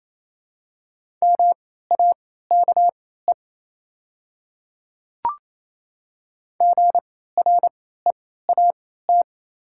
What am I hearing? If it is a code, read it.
Morse code, heard as a keyed tone, about 28 wpm, spaced as for 16 wpm, 700 Hz. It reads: MAKE GREAT
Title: 3x Character Spacing - Top 300 Words - 28wpm